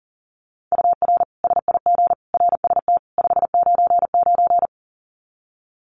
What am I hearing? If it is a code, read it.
Morse code, heard as a keyed tone, about 40 wpm, 700 Hz. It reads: UR SIG RST 599
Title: Common QSO Phrases 40wpm